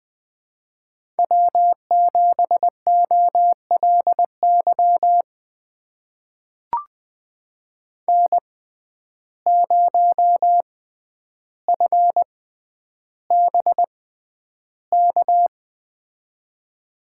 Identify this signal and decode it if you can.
Morse code, heard as a keyed tone, about 20 wpm, spaced as for 10 wpm, 700 Hz. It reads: W7OLY N0FBK
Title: USA Call Signs - 5 Characters - 6x Character Spacing - 20wpm